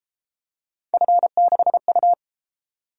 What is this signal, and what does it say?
Morse code, heard as a keyed tone, about 29 wpm, 700 Hz. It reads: F6U